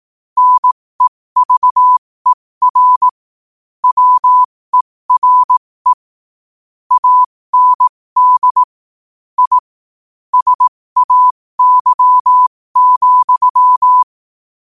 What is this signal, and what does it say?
Morse code, heard as a keyed tone, about 18 wpm, spaced as for 15 wpm, 1000 Hz. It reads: NEVER WERE AND I SAY,
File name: qotdfiles / 15wpm